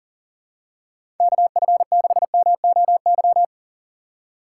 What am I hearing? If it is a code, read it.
Morse code, heard as a keyed tone, about 37 wpm, 700 Hz. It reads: KF6MOY
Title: USA Call Signs - 6 Characters - 2x Character Spacing - 40wpm